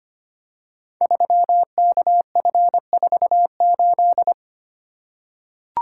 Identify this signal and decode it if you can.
Morse code, heard as a keyed tone, about 25 wpm, 700 Hz. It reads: 3KF48
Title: Code Groups - 5 Letter:Numbers - 25wpm